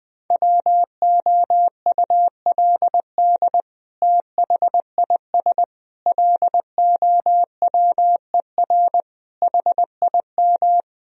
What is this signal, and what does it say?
Morse code, heard as a keyed tone, about 20 wpm, 700 Hz. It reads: WOULD THIS LOWER HIM
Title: Sentences from Top 1000 Words - Review Part 4 - 20wpm